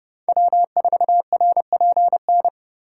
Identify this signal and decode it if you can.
Morse code, heard as a keyed tone, about 30 wpm, 700 Hz. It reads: W4RPN
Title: USA Call Signs - 5 Characters - 8x Character Spacing - 30wpm